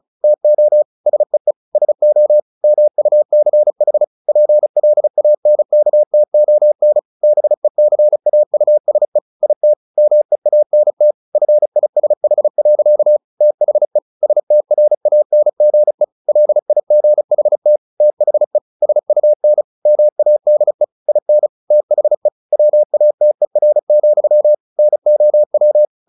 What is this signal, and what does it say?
Morse code, heard as a keyed tone, about 34 wpm, 600 Hz. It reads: TO SEE SO MUCH PLANKTON BECAUSE IT MEANT FISH. THE STRANGE LIGHT THE SUN MADE IN THE WATER, NOW
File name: the old man and the sea / the old man and the sea 1 35wpm